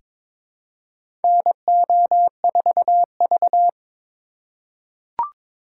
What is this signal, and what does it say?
Morse code, heard as a keyed tone, about 22 wpm, 700 Hz. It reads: NO4V
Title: USA Call Signs - 4 Characters - 3x Character Spacing - 22wpm